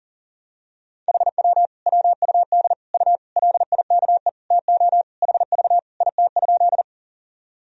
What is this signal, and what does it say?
Morse code, heard as a keyed tone, about 40 wpm, 700 Hz. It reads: HW WUD U LIKE TO HV IT?
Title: Sentences from Top 100 Words and Abbreviations 40wpm